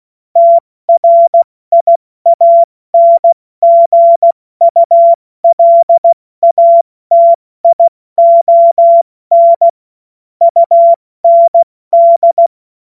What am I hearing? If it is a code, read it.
Morse code, heard as a keyed tone, about 16 wpm, 680 Hz. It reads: TRIANGULATION UND
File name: Morsekurs / Witze / Witz 10.Oktober 2023 70 F80